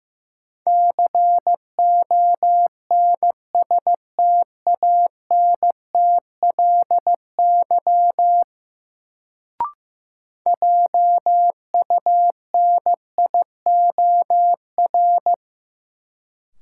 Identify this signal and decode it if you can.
Morse code, heard as a keyed tone, about 15 wpm, 700 Hz. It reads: CONSTANTLY JUNIOR
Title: Sentences from Top 1000 Words - Review Part 7 - New Word Drill - 15wpm